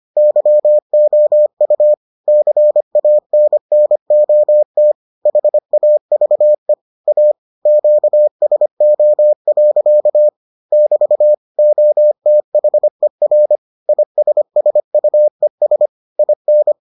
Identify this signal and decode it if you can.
Morse code, heard as a keyed tone, about 25 wpm, 600 Hz. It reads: YOU CANNOT HAVE A QSO. <BT> OTHER ISSUES IN